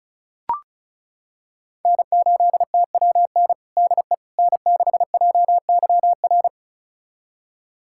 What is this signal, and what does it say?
Morse code, heard as a keyed tone, about 34 wpm, 700 Hz. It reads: N8TWN DE N6JYR
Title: USA Callsign DE Callsign 35wpm